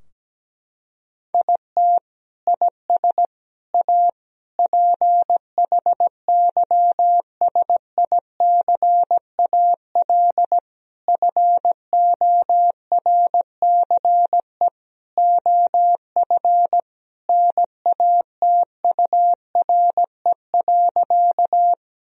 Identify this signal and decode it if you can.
Morse code, heard as a keyed tone, about 17 wpm, 700 Hz. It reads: IT IS A PHYSICAL FORCE OF NATURE.
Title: Sentences from Top 1000 Words - Review Part 4 - 17wpm